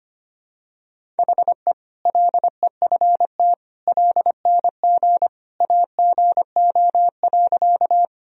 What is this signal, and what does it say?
Morse code, heard as a keyed tone, about 25 wpm, 700 Hz. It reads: HE LEFT LNG AGO.
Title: Sentences from Top 500 Words and Abbreviations 25wpm